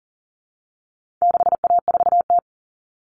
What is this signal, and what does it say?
Morse code, heard as a keyed tone, about 40 wpm, 700 Hz. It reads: 6A4T
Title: All Cut Numbers - 4-Digit 40wpm